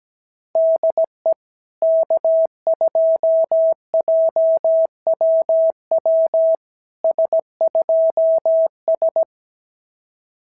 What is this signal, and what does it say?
Morse code, heard as a keyed tone, about 17 wpm, 650 Hz. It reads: DE K2JWW S2S
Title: SOTA QSO Practice 17wpm